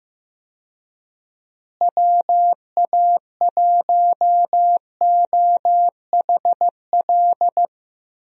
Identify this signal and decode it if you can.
Morse code, heard as a keyed tone, about 15 wpm, 700 Hz. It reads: WA1OHL
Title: USA Call Signs - 6 Characters - Repeated 2x - 15wpm